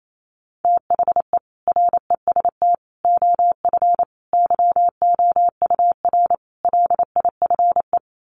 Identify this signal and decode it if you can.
Morse code, heard as a keyed tone, about 28 wpm, 700 Hz. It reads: THE REST OF YOUR LIFE
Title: Sets of 5 Words 28wpm